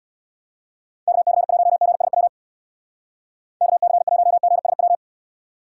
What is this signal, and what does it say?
Morse code, heard as a keyed tone, about 77 wpm, 700 Hz. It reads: KC1GSW KC1GSW